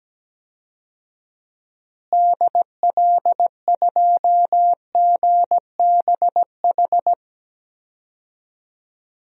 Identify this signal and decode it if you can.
Morse code, heard as a keyed tone, about 17 wpm, 700 Hz. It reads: DL2GBH
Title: DXCC-100 Call Signs - 6 Characters - Repeated 3x - 17wpm